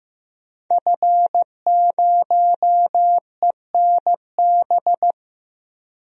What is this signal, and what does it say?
Morse code, heard as a keyed tone, about 15 wpm, 700 Hz. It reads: F0ENB
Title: DXCC-100 Call Signs - 5 Characters - Repeated 3x - 15wpm